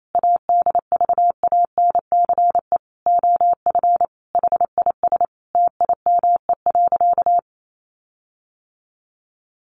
Morse code, heard as a keyed tone, about 28 wpm, 700 Hz. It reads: ADVANCE OF HIS TIME.
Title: Sentences from Top 1400 Words 28wpm